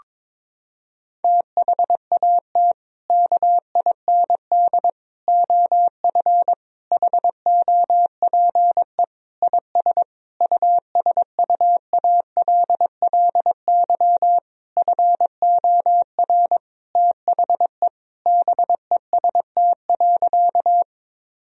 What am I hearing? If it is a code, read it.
Morse code, heard as a keyed tone, about 22 wpm, 700 Hz. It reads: THAT KIND OF HOPE IS USUALLY FOR THE BEST.
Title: Sentences from Top 1000 Words - Review Part 3 - 22wpm